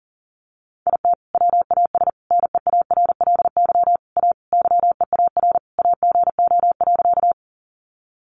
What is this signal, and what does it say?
Morse code, heard as a keyed tone, about 40 wpm, 700 Hz. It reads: IT WAS NEARLY A YEAR AGO.
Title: Sentences from Top 1300 Words 40wpm